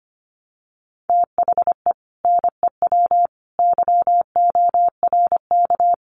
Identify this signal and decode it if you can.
Morse code, heard as a keyed tone, about 25 wpm, 700 Hz. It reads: THE NEW YORK
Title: Sets of 3 Words 25wpm